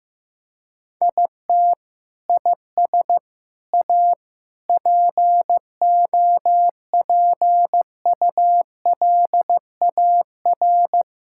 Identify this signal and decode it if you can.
Morse code, heard as a keyed tone, about 15 wpm, 700 Hz. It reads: IT IS A POPULAR
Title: Sentences from Top 1000 Words - Review Part 5 - 15wpm